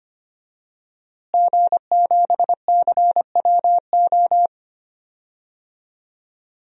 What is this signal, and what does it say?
Morse code, heard as a keyed tone, about 25 wpm, 700 Hz. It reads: G7CWO
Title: DXCC-100 Call Signs - 5 Characters - Repeated 2x - 25wpm